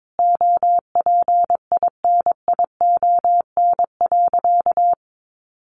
Morse code, heard as a keyed tone, about 22 wpm, 700 Hz. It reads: OPINION.
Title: Sentences from Top 1200 Words 22wpm